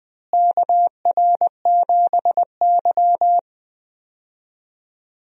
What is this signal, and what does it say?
Morse code, heard as a keyed tone, about 20 wpm, 700 Hz. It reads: KR7Y
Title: USA Call Signs - 4 Characters - Repeated 3x - 20wpm